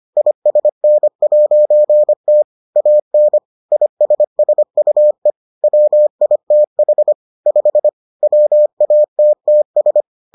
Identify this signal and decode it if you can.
Morse code, heard as a keyed tone, about 25 wpm, 600 Hz. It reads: ISN'T AN ISSUE WITH 5 WATTS